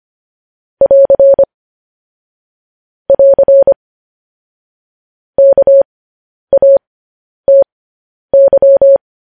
Morse code, heard as a keyed tone, about 25 wpm, spaced as for 10 wpm, 550 Hz. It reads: <AR> <AR> KATY
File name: podcast / News headlines-1x-25x10